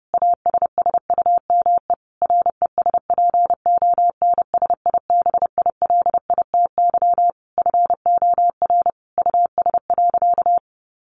Morse code, heard as a keyed tone, about 30 wpm, 700 Hz. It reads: ASSUME RESPONSIBILITY FOR US.
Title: Sentences from Top 1500 Words 30wpm